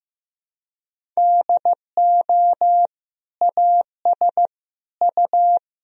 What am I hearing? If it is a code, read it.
Morse code, heard as a keyed tone, about 15 wpm, 700 Hz. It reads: DO AS U
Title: Sentences from Top 100 Words and Abbreviations 15wpm